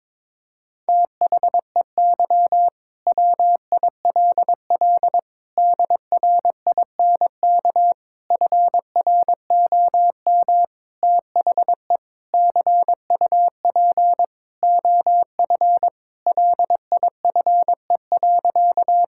Morse code, heard as a keyed tone, about 21 wpm, 700 Hz. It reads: THEY WILL DRINK FROM THE CUP OF LIFE.
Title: Sentences from Top 1600 Words 22wpm